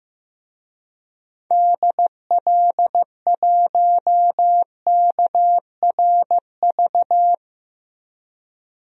Morse code, heard as a keyed tone, about 15 wpm, 700 Hz. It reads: DL1KRV
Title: DXCC-100 Call Signs - 6 Characters - 15wpm